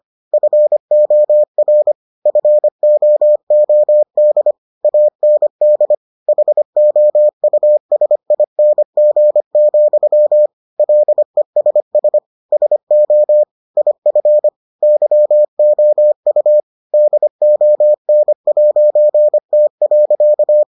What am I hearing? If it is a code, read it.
Morse code, heard as a keyed tone, about 25 wpm, 600 Hz. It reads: FOR FOOD AND HOUSING, LESS SO IF YOU DON'T.